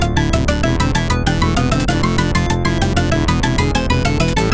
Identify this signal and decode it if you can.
There is no code.